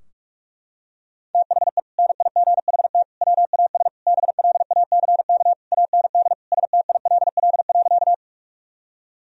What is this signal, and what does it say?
Morse code, heard as a keyed tone, about 41 wpm, 700 Hz. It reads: THE NIGHT WAS BLACK AND STILL.
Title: Sentences from Top 1000 Words - Review Part 2 - 45wpm